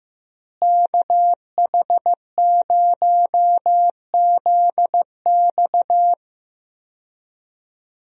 Morse code, heard as a keyed tone, about 16 wpm, 700 Hz. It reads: KH0ZX
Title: USA Call Signs - 5 Characters - Repeated 3x - 15wpm